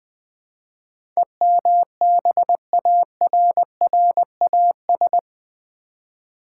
Morse code, heard as a keyed tone, about 20 wpm, 700 Hz. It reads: EMBARRAS
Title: Build A Word - 4 Letters At A Time - 11-Letter Words - 20wpm